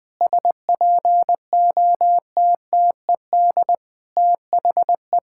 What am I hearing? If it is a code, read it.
Morse code, heard as a keyed tone, about 20 wpm, 700 Hz. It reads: SPOTTED THE
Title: Pangram Challenge v2 - 20wpm